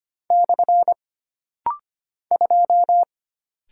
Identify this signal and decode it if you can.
Morse code, heard as a keyed tone, about 25 wpm, 700 Hz. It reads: / 2
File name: Single Character (Letter:Number:Punctuation:Prosign) - Rapid-Fire - 25wpm